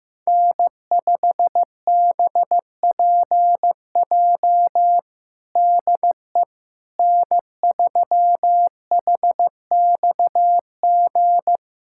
Morse code, heard as a keyed tone, about 16 wpm, 700 Hz. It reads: N5BPJ DE N3HXG
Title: K1USN SST QSO Practice 15wpm